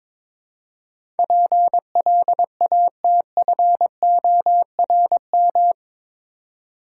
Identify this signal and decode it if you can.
Morse code, heard as a keyed tone, about 22 wpm, 700 Hz. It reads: PLATFORM
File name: practice / 22wpm / Eight Letter Words - 22wpm